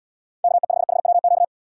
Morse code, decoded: KE5HWQ

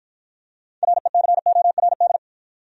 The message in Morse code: RECORD

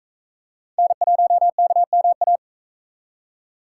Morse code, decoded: N1KMA

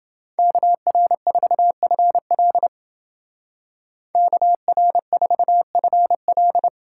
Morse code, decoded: KR4FL KR4FL

